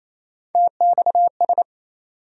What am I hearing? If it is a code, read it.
Morse code, heard as a keyed tone, about 28 wpm, 700 Hz. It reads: TXS
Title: USA Call Sign Postfixes - 3 Letters 28wpm